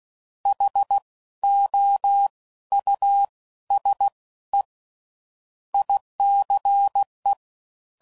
Morse code, heard as a keyed tone, about 16 wpm, 800 Hz. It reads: H O U S E ICE
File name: Suffixes / General / Houses